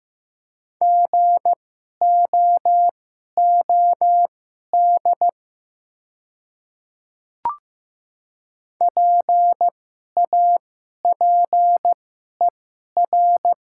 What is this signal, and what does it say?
Morse code, heard as a keyed tone, about 15 wpm, spaced as for 11 wpm, 700 Hz. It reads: GOOD PAPER
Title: 2x Character Spacing - Top 300 Words - 15wpm